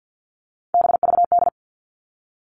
Morse code, heard as a keyed tone, about 50 wpm, 700 Hz. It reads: BVD
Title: All Cut Numbers - 3-Digit 50wpm